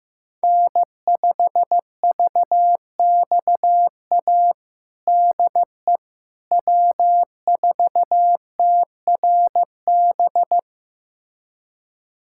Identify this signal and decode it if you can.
Morse code, heard as a keyed tone, about 15 wpm, 700 Hz. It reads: N5VXA DE W4TRB
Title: USA Callsign DE Callsign - 5 Characters - 15wpm